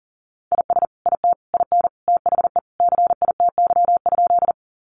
Morse code, heard as a keyed tone, about 40 wpm, 700 Hz. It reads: IS IT IN THE CITY?
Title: Sentences from Top 300 Words 40wpm